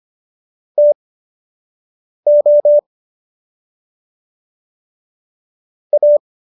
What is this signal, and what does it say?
Morse code, heard as a keyed tone, about 25 wpm, spaced as for 6 wpm, 590 Hz. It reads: TO A